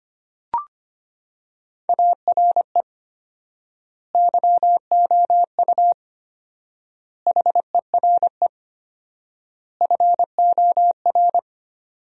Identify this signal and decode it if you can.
Morse code, heard as a keyed tone, about 25 wpm, 700 Hz. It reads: ARE YOU HERE FOR
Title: 4x Word-Spacing - Sentences from Top 200 Words - 25wpm